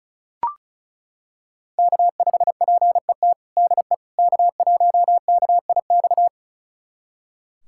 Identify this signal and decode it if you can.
Morse code, heard as a keyed tone, about 36 wpm, 700 Hz. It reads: K5PET DE K1KIX
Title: USA Callsign DE Callsign - 5 Characters - 35wpm